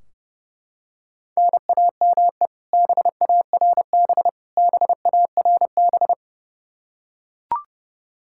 Morse code, heard as a keyed tone, about 32 wpm, 700 Hz. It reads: NAME BARB BARB
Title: Practice QSO - Exchange 1 - Beginning - 30wpm